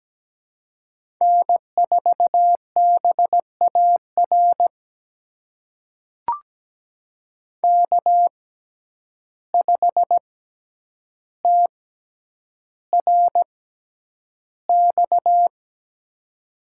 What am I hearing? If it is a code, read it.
Morse code, heard as a keyed tone, about 17 wpm, spaced as for 9 wpm, 700 Hz. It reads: N4BAR K5TRX
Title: USA Call Signs - 5 Characters - 6x Character Spacing - 17wpm